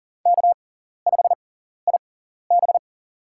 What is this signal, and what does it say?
Morse code, heard as a keyed tone, about 40 wpm, spaced as for 14 wpm, 700 Hz. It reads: K5IB
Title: USA Call Signs - 4 Characters - 6x Character Spacing - 40wpm